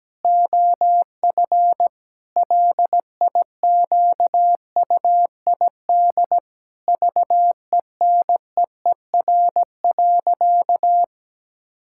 Morse code, heard as a keyed tone, about 17 wpm, 700 Hz. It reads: OF LIQUID VENEER.
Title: Pangram Challenge v2 - 17wpm